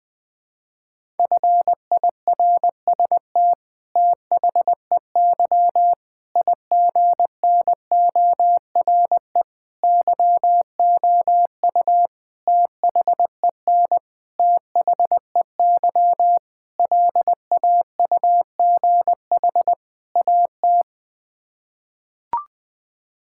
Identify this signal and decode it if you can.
Morse code, heard as a keyed tone, about 21 wpm, 700 Hz. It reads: FIRST THEY IGNORE YOU THEN THEY LAUGH AT
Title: Sets of 8 Words - 20wpm